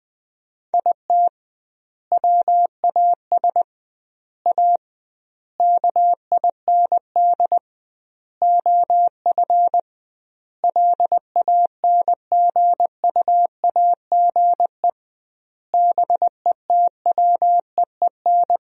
Morse code, heard as a keyed tone, about 20 wpm, 700 Hz. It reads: IT WAS A KIND OF LANGUAGE BETWEEN
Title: 2x Word-Spacing - Sentences from Top 500 Words - 20wpm